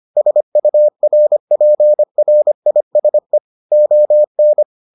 Morse code, heard as a keyed tone, about 25 wpm, 600 Hz. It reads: SURPRISE ON